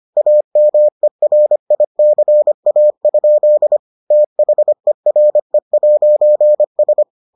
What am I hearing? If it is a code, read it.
Morse code, heard as a keyed tone, about 25 wpm, 600 Hz. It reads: AMERICA? THERE'S